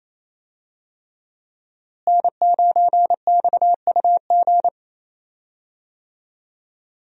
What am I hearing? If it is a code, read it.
Morse code, heard as a keyed tone, about 28 wpm, 700 Hz. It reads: N9XUG
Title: USA Call Signs - 5 Characters - Repeated 3x - 28wpm